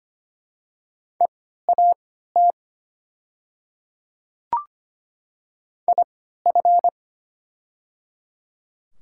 Morse code, heard as a keyed tone, about 25 wpm, spaced as for 14 wpm, 690 Hz. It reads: EAT IF
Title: 3x Character Spacing - Top 300 Words - 25wpm